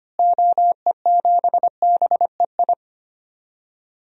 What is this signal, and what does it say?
Morse code, heard as a keyed tone, about 25 wpm, 700 Hz. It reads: OE7BEI
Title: DXCC-100 Call Signs - 6 Characters - Repeated 2x - 25wpm